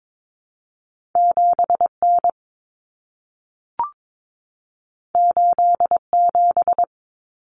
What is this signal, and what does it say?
Morse code, heard as a keyed tone, about 22 wpm, 700 Hz. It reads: 7N 87
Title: Common Cut Numbers - 2-Digit 22wpm